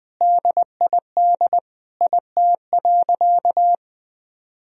Morse code, heard as a keyed tone, about 20 wpm, 700 Hz. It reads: DID IT.
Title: Sentences from Top 1000 Words - Review Part 4 - 20wpm